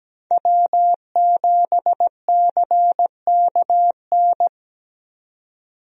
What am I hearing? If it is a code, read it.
Morse code, heard as a keyed tone, about 17 wpm, 700 Hz. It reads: W7CKN
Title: USA Call Signs - 5 Characters - 17wpm